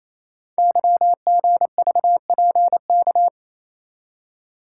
Morse code, heard as a keyed tone, about 29 wpm, 700 Hz. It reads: YGVPK
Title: Code Groups - 5 Letters - 28wpm